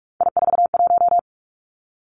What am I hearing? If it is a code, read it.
Morse code, heard as a keyed tone, about 46 wpm, 700 Hz. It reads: I41
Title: Three Challenging Letter-Number Combinations 45wpm